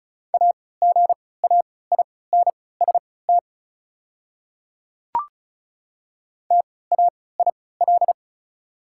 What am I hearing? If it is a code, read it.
Morse code, heard as a keyed tone, about 36 wpm, spaced as for 20 wpm, 690 Hz. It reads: AGAINST TAIL
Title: 3x Character Spacing - Top 500 Words - 35wpm